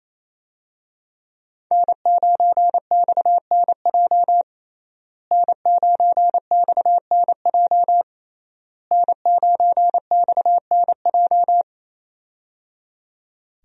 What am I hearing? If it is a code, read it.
Morse code, heard as a keyed tone, about 28 wpm, 700 Hz. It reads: N9XNJ N9XNJ N9XNJ
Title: USA Call Signs - 5 Characters - Repeated 3x - 28wpm